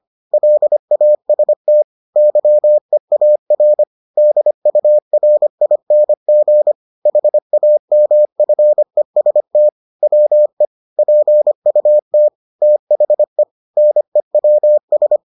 Morse code, heard as a keyed tone, about 25 wpm, 600 Hz. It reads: LAST YEAR DURING HAMFEST WE PUT THE NEWS